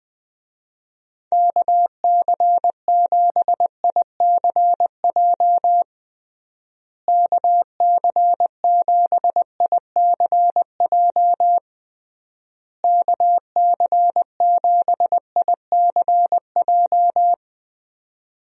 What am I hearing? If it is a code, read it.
Morse code, heard as a keyed tone, about 20 wpm, 700 Hz. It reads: KC7ICJ KC7ICJ KC7ICJ
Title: USA Call Signs - 6 Characters - Repeated 3x - 20wpm